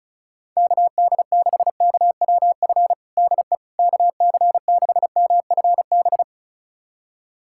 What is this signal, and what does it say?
Morse code, heard as a keyed tone, about 34 wpm, 700 Hz. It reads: KD6KWF DE KC6MFB
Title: USA Callsign DE Callsign - 6 Characters - 35wpm